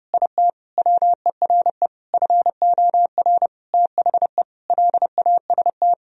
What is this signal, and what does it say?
Morse code, heard as a keyed tone, about 32 wpm, 700 Hz. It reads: IT WERE FOR THE LAST
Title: Sentences from Top 1000 Words - Review Part 2 - 30wpm